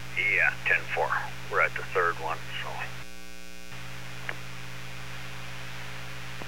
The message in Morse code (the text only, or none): none